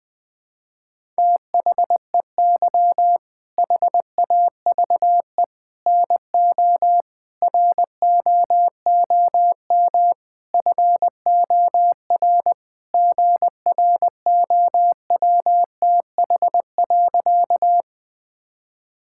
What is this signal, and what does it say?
Morse code, heard as a keyed tone, about 21 wpm, 700 Hz. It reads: THEY HAVE NO ROOM FOR GROWTH.
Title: Sentences from Top 1000 Words - Review Part 5 - 20wpm